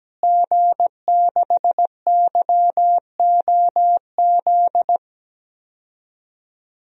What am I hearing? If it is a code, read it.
Morse code, heard as a keyed tone, about 17 wpm, 700 Hz. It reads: G6YOZ
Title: DXCC-100 Call Signs - 5 Characters - Repeated 2x - 17wpm